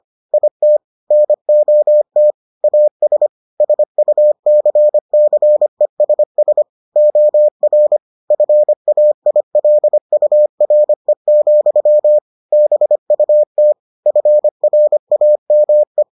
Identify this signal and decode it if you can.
Morse code, heard as a keyed tone, about 25 wpm, 600 Hz. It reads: IT NOT AS SUCCESS OR FAILURE, BUT FRAME